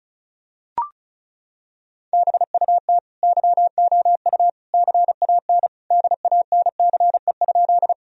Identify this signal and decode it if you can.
Morse code, heard as a keyed tone, about 34 wpm, 700 Hz. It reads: BUT YOU CAN DANCE?
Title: Sentences from Top 1000 Words - Review Part 4 - 35wpm